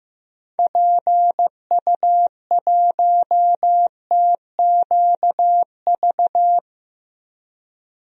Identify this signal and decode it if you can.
Morse code, heard as a keyed tone, about 15 wpm, 700 Hz. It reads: PU1TQV
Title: DXCC-100 Call Signs - 6 Characters - 15wpm